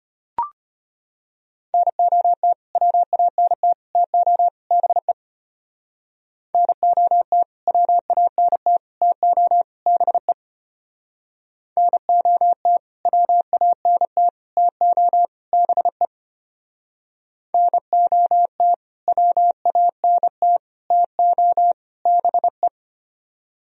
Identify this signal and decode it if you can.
Morse code, heard as a keyed tone, about 32 wpm, 700 Hz. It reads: NOT WANT TO BE NOT WANT TO BE NOT WANT TO BE NOT WANT TO BE